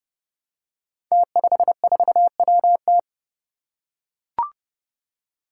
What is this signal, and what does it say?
Morse code, heard as a keyed tone, about 30 wpm, 700 Hz. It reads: T54WT
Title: Five Challenging Letter-Number Combinations 30wpm